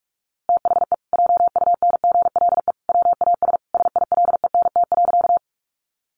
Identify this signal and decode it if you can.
Morse code, heard as a keyed tone, about 44 wpm, 700 Hz. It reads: THE JUNGLE WAS SILENT.
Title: A Sound Of Thunder 45wpm